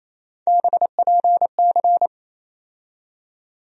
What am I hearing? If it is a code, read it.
Morse code, heard as a keyed tone, about 28 wpm, 700 Hz. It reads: BPC
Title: USA Call Sign Postfixes - 3 Letters 28wpm